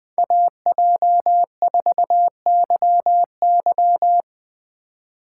Morse code, heard as a keyed tone, about 20 wpm, 700 Hz. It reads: AJ4YY
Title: USA Call Signs - 6x Character Spacing - 20wpm